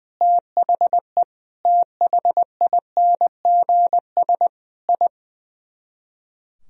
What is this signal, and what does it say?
Morse code, heard as a keyed tone, about 20 wpm, 700 Hz. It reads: THE THINGS I
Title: Speed-Racing - Sets of 5 Words 20wpm